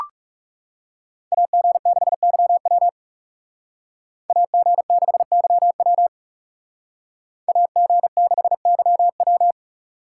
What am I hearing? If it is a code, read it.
Morse code, heard as a keyed tone, about 36 wpm, 700 Hz. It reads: AG6YW AG6YW AG6YW
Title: Speed Racing - USA 2x2 Call Signs 30wpm